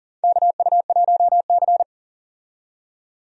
Morse code, heard as a keyed tone, about 40 wpm, 700 Hz. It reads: KU1C
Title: USA Call Signs - 4 Characters - 40wpm